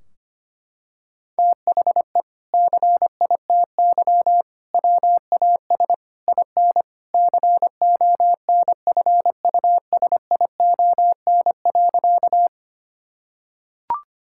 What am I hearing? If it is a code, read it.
Morse code, heard as a keyed tone, about 25 wpm, 700 Hz. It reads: THE CITY WAS IN CONFUSION.
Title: Sentences from Top 1600 Words 25wpm